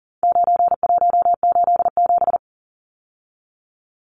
Morse code, heard as a keyed tone, about 40 wpm, 700 Hz. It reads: 9187
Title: Numbers - 4-Digit 40wpm